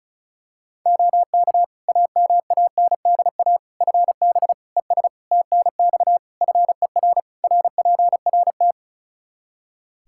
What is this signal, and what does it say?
Morse code, heard as a keyed tone, about 36 wpm, 700 Hz. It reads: OK AMANDA FB ES TNX FER RPRT
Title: K1USN SST QSO Practice 35wpm